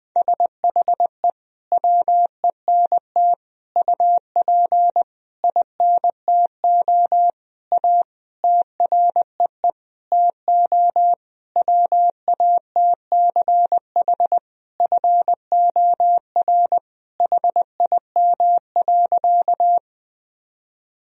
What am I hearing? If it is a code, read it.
Morse code, heard as a keyed tone, about 20 wpm, 700 Hz. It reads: SHE WENT UP INTO A TREE TO WATCH FOR HIM.
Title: Sentences from Top 1000 Words - Review Part 1 - 20wpm